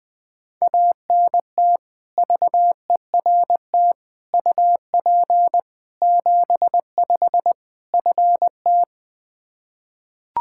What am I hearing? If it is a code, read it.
Morse code, heard as a keyed tone, about 20 wpm, 700 Hz. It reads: ANT VERT UP 75 FT
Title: Practice QSO - Exchange 2 - Info - 20wpm